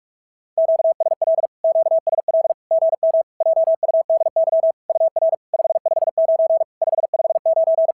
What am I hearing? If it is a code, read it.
Morse code, heard as a keyed tone, about 46 wpm, 650 Hz. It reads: QSL QSL GM JUDY UR 559 559
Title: SOTA QSO Practice 45wpm